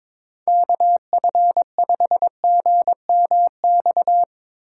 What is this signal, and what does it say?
Morse code, heard as a keyed tone, about 21 wpm, 700 Hz. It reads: KF5GMX
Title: USA Call Signs - 6 Characters - 3x Character Spacing - 22wpm